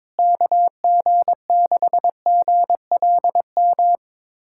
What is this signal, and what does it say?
Morse code, heard as a keyed tone, about 21 wpm, 700 Hz. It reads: KG6GLM
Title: USA Callsign DE Callsign - 6 Characters - 22wpm